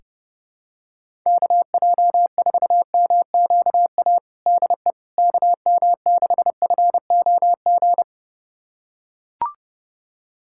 Morse code, heard as a keyed tone, about 30 wpm, 700 Hz. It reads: KJ4MQA DE KM6FOG
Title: USA Callsign DE Callsign - 6 Characters - 30wpm